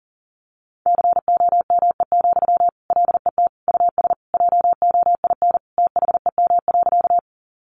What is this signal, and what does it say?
Morse code, heard as a keyed tone, about 37 wpm, 700 Hz. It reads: COME, LET US JOIN THEM.
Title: Sentences from Top 1400 Words 40wpm